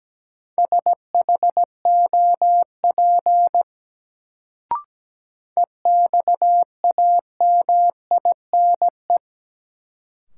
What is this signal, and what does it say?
Morse code, heard as a keyed tone, about 17 wpm, 700 Hz. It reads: SHOP EXAMINE
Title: Sentences from Top 1600 Words - New Word Drill Alt - 17wpm